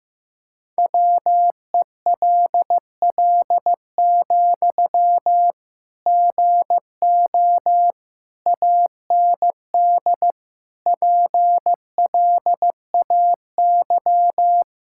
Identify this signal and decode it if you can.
Morse code, heard as a keyed tone, about 15 wpm, 700 Hz. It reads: WELL, GO AND PLAY
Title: Sentences from Top 1600 Words 15wpm